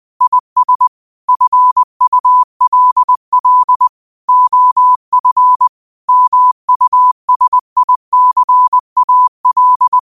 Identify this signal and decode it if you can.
Morse code, heard as a keyed tone, about 20 wpm, 1000 Hz. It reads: IS FULL OF MUSICAL
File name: qotdfiles / 20wpm